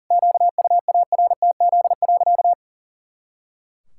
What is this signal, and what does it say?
Morse code, heard as a keyed tone, about 40 wpm, 700 Hz. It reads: QUARTZ.
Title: Pangram Challenge v2 - 40wpm